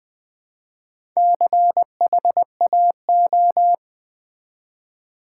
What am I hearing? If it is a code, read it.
Morse code, heard as a keyed tone, about 20 wpm, 700 Hz. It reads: CHAO
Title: Build A Word - 1 Letter At A Time - 3 to 5-Letter Words - 20wpm